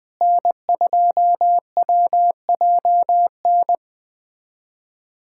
Morse code, heard as a keyed tone, about 20 wpm, 700 Hz. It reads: N2WJN
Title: USA Callsign DE Callsign - 5 Characters - 20wpm